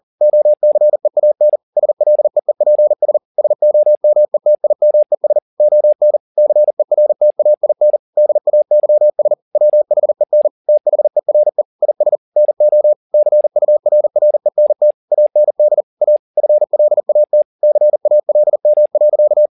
Morse code, heard as a keyed tone, about 40 wpm, 600 Hz. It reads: OCEAN SLEEPS SOMETIMES ON CERTAIN DAYS WHEN THERE IS NO CURRENT AND A FLAT CALM.
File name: the old man and the sea 2 40wpm